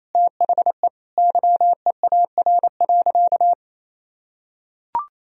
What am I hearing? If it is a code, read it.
Morse code, heard as a keyed tone, about 28 wpm, 700 Hz. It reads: THE YEAR.
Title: Sentences from Top 1000 Words - Review Part 1 - 28wpm